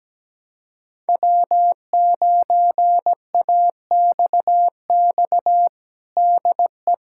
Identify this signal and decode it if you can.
Morse code, heard as a keyed tone, about 17 wpm, 700 Hz. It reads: W9AXX DE
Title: USA Callsign DE Callsign - 5 Characters - 17wpm